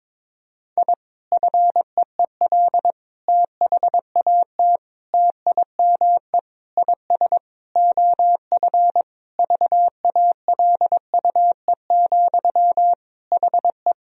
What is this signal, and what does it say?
Morse code, heard as a keyed tone, about 22 wpm, 700 Hz. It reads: I FEEL THAT TIME IS OF VALUE, HE
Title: Sentences from Top 1000 Words - Review Part 3 - 22wpm